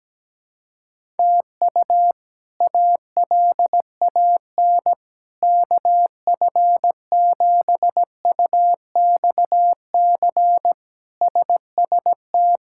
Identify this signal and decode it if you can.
Morse code, heard as a keyed tone, about 17 wpm, 700 Hz. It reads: TU ALAN KF7UXC SST
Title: K1USN SST QSO Practice - 17wpm